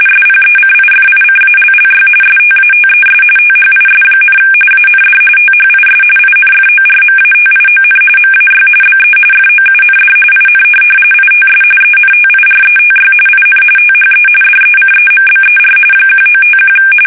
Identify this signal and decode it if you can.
RTTY, about 45.45 baud. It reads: SPECIFICALLY STATES THIS, AND THIS INFORMATION MAY BE FOUND IN ITS ENTIRETY IN THE DECEMBER 1951
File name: fsk850 / teletype-RTTY